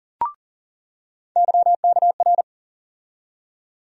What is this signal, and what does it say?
Morse code, heard as a keyed tone, about 40 wpm, 700 Hz. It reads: YKR
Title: USA Call Sign Postfixes - 3 Letters 40wpm